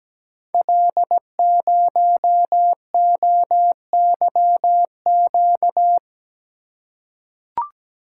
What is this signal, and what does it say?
Morse code, heard as a keyed tone, about 17 wpm, 700 Hz. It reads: L0OYQ
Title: Code Groups - 5 Letter:Numbers - 17wpm